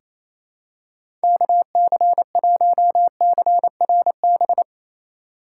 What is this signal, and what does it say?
Morse code, heard as a keyed tone, about 28 wpm, 700 Hz. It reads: KC1CRB
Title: USA Call Signs - 6 Characters - 4x Character Spacing - 28wpm